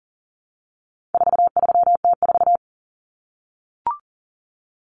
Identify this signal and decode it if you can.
Morse code, heard as a keyed tone, about 40 wpm, 700 Hz. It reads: 43T4
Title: Common Cut Numbers - 4-Digit 40wpm